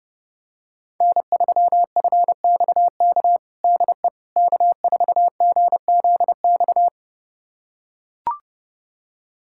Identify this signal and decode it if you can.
Morse code, heard as a keyed tone, about 32 wpm, 700 Hz. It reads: N3FXK DE K4GZX
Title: USA Callsign DE Callsign - 5 Characters - 30wpm